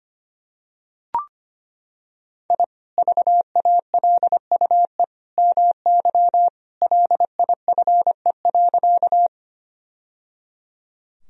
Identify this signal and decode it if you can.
Morse code, heard as a keyed tone, about 25 wpm, 700 Hz. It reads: I VALUE MY LIFE.
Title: Sentences from Top 1000 Words - Review Part 3 - 25wpm